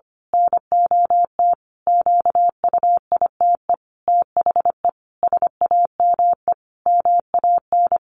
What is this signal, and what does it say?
Morse code, heard as a keyed tone, about 25 wpm, 700 Hz. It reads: NOT QUITE THE SAME MAN
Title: A Sound Of Thunder 25wpm